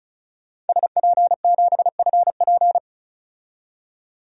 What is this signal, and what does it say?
Morse code, heard as a keyed tone, about 34 wpm, 700 Hz. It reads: SP7FP
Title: DXCC-100 Call Signs - 5 Characters - 35wpm